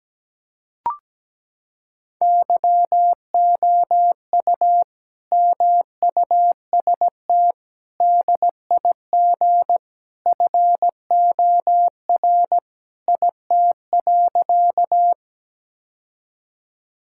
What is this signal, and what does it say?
Morse code, heard as a keyed tone, about 17 wpm, 700 Hz. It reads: YOU MUST DIG FOR IT.
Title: Sentences from Top 1600 Words 17wpm